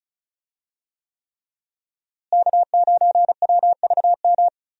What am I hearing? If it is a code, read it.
Morse code, heard as a keyed tone, about 34 wpm, 700 Hz. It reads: K9WVM
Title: USA Call Signs - 5 Characters - Repeated 3x - 35wpm